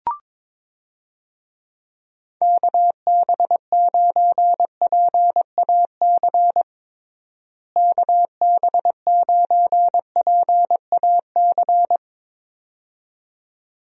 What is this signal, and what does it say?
Morse code, heard as a keyed tone, about 22 wpm, 700 Hz. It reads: KB9PAC KB9PAC
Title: USA Call Signs - 6 Characters - Repeated 2x - 22wpm